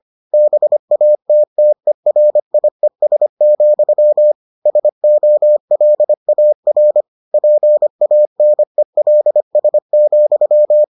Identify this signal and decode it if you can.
Morse code, heard as a keyed tone, about 25 wpm, 600 Hz. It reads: BATTERIES, SOLAR PANELS,